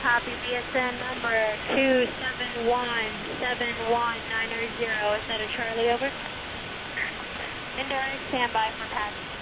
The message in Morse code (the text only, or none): none